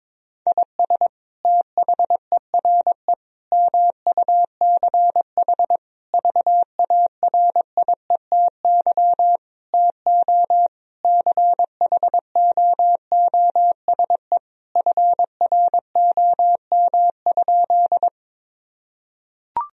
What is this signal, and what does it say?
Morse code, heard as a keyed tone, about 22 wpm, 700 Hz. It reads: IS THERE MUCH VARIETY TO CHOOSE FROM?
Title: Sentences from Top 1000 Words - Review Part 4 - 22wpm